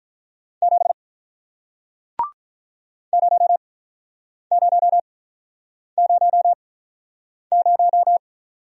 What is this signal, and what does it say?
Morse code, heard as a keyed tone, about 48 wpm, spaced as for 9 wpm, 700 Hz. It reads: 7 0000